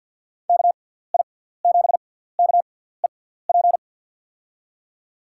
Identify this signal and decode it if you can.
Morse code, heard as a keyed tone, about 51 wpm, spaced as for 17 wpm, 700 Hz. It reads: KI7KEP
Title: USA Call Signs - 6 Characters - 6x Character Spacing - 50wpm